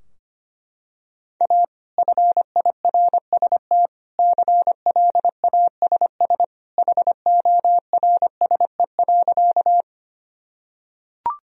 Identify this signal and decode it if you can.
Morse code, heard as a keyed tone, about 25 wpm, 700 Hz. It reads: A FIRST CLASS HORSE.
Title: Sentences from Top 1000 Words - Review Part 2 - 25wpm